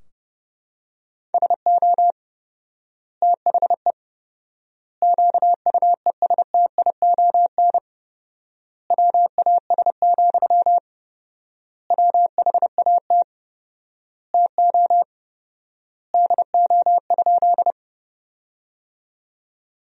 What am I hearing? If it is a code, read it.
Morse code, heard as a keyed tone, about 32 wpm, 700 Hz. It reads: SO THE QUESTION WAS, WHAT TO DO?
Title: 4x Word-Spacing - Sentences from Top 400 Words - 30wpm